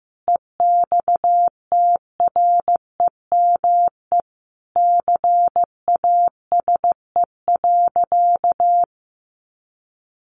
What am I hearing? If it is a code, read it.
Morse code, heard as a keyed tone, about 15 wpm, 700 Hz. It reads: EXTREME CASE.
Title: Sentences from Top 1300 Words 15wpm